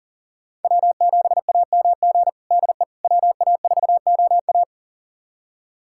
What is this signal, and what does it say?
Morse code, heard as a keyed tone, about 40 wpm, 700 Hz. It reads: W7AMG DE WA4OA
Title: USA Callsign DE Callsign - 5 Characters - 40wpm